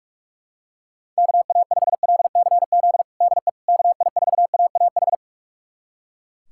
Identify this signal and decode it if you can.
Morse code, heard as a keyed tone, about 41 wpm, 700 Hz. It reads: KA5LCZ DE KI4AAH
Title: USA Callsign DE Callsign - 6 Characters - 45wpm